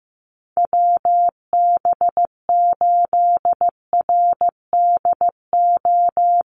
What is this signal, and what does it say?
Morse code, heard as a keyed tone, about 15 wpm, 700 Hz. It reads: WB8RDO
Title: USA Call Signs Repeated 2x 15wpm